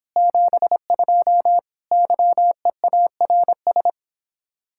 Morse code, heard as a keyed tone, about 29 wpm, 700 Hz. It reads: 72 YEARS